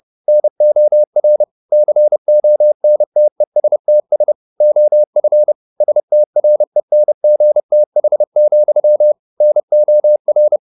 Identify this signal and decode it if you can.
Morse code, heard as a keyed tone, about 30 wpm, 600 Hz. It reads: NOR CONTESTS OF STRENGTH, NOR
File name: the old man and the sea / the old man and the sea 1 30wpm